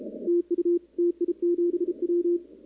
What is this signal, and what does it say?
Morse code, heard as a keyed tone, about 32 wpm, 350 Hz, with radio noise. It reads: TU TI7W
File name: contests / 2019-cw-wpx-kt5j-k5pi- -2330 0-11